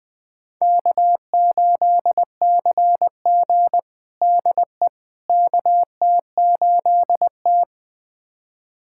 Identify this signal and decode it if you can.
Morse code, heard as a keyed tone, about 20 wpm, 700 Hz. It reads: K8CG DE KT8T
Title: USA Callsign DE Callsign - 4 Characters - 20wpm